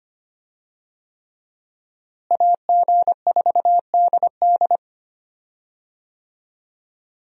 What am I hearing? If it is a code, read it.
Morse code, heard as a keyed tone, about 25 wpm, 700 Hz. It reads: AG4DD
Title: USA Call Signs - 5 Characters - Repeated 3x - 25wpm